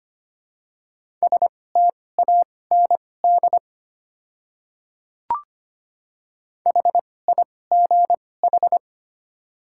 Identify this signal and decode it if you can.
Morse code, heard as a keyed tone, about 25 wpm, spaced as for 18 wpm, 700 Hz. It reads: STAND HIGH
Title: 2x Character Spacing - Top 300 Words - 25wpm